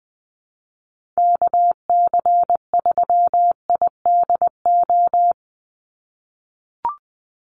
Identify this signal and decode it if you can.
Morse code, heard as a keyed tone, about 20 wpm, 700 Hz. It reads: KC3IDO
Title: USA Call Signs 20wpm